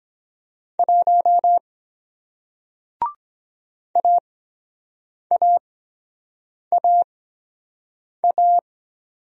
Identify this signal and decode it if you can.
Morse code, heard as a keyed tone, about 24 wpm, spaced as for 7 wpm, 700 Hz. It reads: 1 AAAA